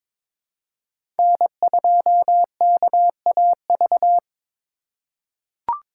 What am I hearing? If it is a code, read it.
Morse code, heard as a keyed tone, about 22 wpm, 700 Hz. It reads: N2KAV
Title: USA Call Signs - 5 Characters - 2x Character Spacing - 22wpm